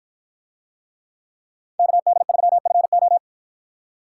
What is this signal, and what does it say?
Morse code, heard as a keyed tone, about 50 wpm, 700 Hz. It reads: KD3FO